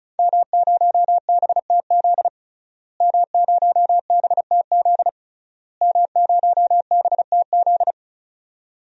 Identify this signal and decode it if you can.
Morse code, heard as a keyed tone, about 34 wpm, 700 Hz. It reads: M0BTZ M0BTZ M0BTZ
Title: DXCC-100 Call Signs - 5 Characters - Repeated 3x - 35wpm